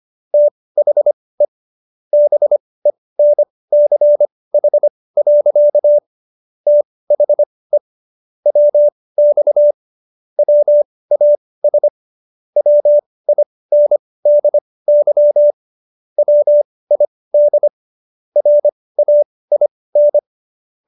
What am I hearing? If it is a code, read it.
Morse code, heard as a keyed tone, about 25 wpm, spaced as for 18 wpm, 590 Hz. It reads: THE BENCH. THE WX WAS WINDY WID RAIN